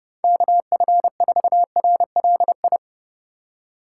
Morse code, heard as a keyed tone, about 38 wpm, 700 Hz. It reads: KF4RLI